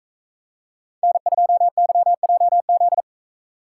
Morse code, heard as a keyed tone, about 40 wpm, 700 Hz. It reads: N2YJZ